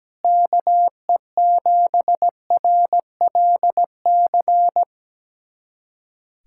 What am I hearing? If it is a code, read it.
Morse code, heard as a keyed tone, about 17 wpm, 700 Hz. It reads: KE7RLC
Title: USA Callsign DE Callsign 17wpm